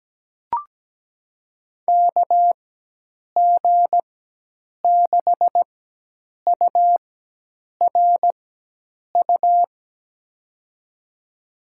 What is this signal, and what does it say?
Morse code, heard as a keyed tone, about 17 wpm, spaced as for 8 wpm, 700 Hz. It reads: KG6URU
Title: USA Call Signs - 4x Character Spacing - 17wpm